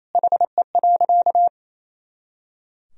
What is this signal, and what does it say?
Morse code, heard as a keyed tone, about 28 wpm, 700 Hz. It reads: HE.
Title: Sentences from Top 100 Words and Abbreviations 28wpm